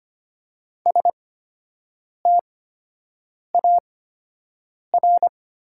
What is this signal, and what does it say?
Morse code, heard as a keyed tone, about 25 wpm, spaced as for 7 wpm, 700 Hz. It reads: STAR
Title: 8x Character Spacing - Top 500 Words - 25wpm